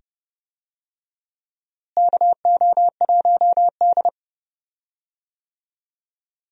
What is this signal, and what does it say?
Morse code, heard as a keyed tone, about 30 wpm, 700 Hz. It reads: KO1D
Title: USA Call Signs - 4 Characters - Repeated 2x - 30wpm